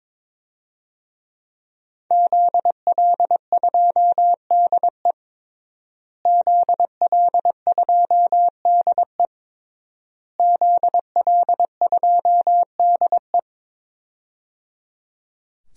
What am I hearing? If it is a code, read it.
Morse code, heard as a keyed tone, about 21 wpm, 700 Hz. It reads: ZL2DE ZL2DE ZL2DE
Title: DXCC-100 Call Signs - 5 Characters - Repeated 3x - 22wpm